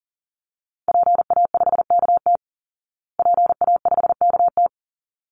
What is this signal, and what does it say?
Morse code, heard as a keyed tone, about 40 wpm, 700 Hz. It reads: PA5KT PA5KT
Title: Call Signs Repeated 3x - 40wpm